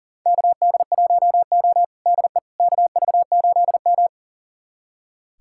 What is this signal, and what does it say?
Morse code, heard as a keyed tone, about 37 wpm, 700 Hz. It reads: KD1O DE KV8M
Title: USA Callsign DE Callsign - 4 Characters - 40wpm